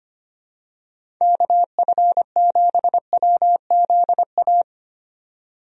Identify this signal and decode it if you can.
Morse code, heard as a keyed tone, about 25 wpm, 700 Hz. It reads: KF7WZA
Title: USA Call Signs - 6 Characters - 25wpm